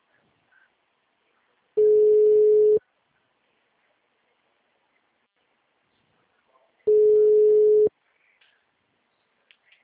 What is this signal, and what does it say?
Ringback tone (Europe, 425 Hz).
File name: out- -2001- - - .70227